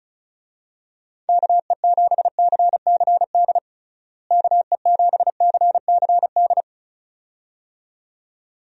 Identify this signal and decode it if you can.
Morse code, heard as a keyed tone, about 36 wpm, 700 Hz. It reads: KE7CCD KE7CCD
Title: USA Call Signs - 6 Characters - Repeated 2x - 35wpm